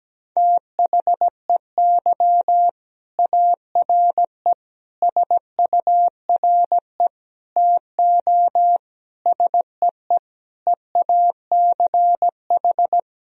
Morse code, heard as a keyed tone, about 17 wpm, 700 Hz. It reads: THEY ARE SURE TO SEE EACH
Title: Sentences from Top 1000 Words - Review Part 1 - 17wpm